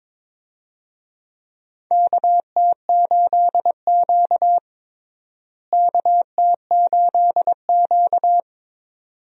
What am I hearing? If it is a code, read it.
Morse code, heard as a keyed tone, about 22 wpm, 700 Hz. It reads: KT8Q KT8Q
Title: USA Call Signs - 4 Characters - Repeated 2x - 22wpm